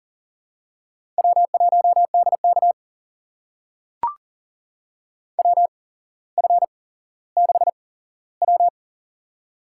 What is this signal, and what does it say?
Morse code, heard as a keyed tone, about 42 wpm, spaced as for 17 wpm, 690 Hz. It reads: W1DK WF6W
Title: USA Call Signs - 4 Characters - 8x Character Spacing - 40wpm